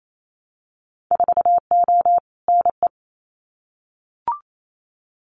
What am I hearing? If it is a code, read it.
Morse code, heard as a keyed tone, about 28 wpm, 700 Hz. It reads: 4O NE
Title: Winter Field Day Exchanges Encore 28wpm